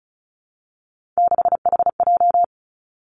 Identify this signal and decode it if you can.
Morse code, heard as a keyed tone, about 36 wpm, 700 Hz. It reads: 6HJ
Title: Three Challenging Letter-Number Combinations 35wpm